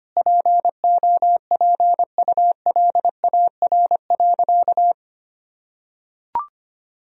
Morse code, heard as a keyed tone, about 25 wpm, 700 Hz. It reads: POPULAR.
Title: Sentences from Top 1000 Words - Review Part 3 - 25wpm